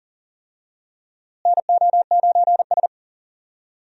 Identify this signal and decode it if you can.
Morse code, heard as a keyed tone, about 40 wpm, 700 Hz. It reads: NO9S
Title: USA Call Signs - 4 Characters - Repeated 3x - 40wpm